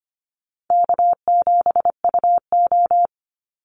Morse code, heard as a keyed tone, about 25 wpm, 700 Hz. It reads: K7UO
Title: USA Call Signs 25wpm